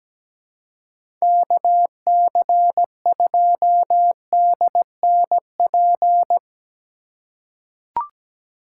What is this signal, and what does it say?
Morse code, heard as a keyed tone, about 17 wpm, 700 Hz. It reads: KC2DNP
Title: USA Call Signs - 2x Character Spacing - 17wpm